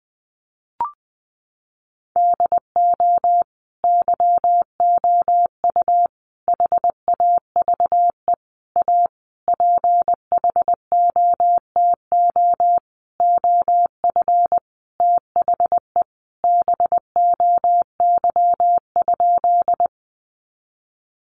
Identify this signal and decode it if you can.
Morse code, heard as a keyed tone, about 20 wpm, 700 Hz. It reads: DO YOU HAVE A PHOTO OF THE BOY?
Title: Sentences from Top 1200 Words 20wpm